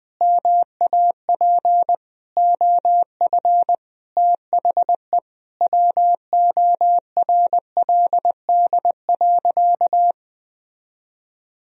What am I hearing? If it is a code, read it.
Morse code, heard as a keyed tone, about 19 wpm, 700 Hz. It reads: MAP OF THE WORLD.
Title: Sentences from Top 1000 Words - Review Part 2 - 20wpm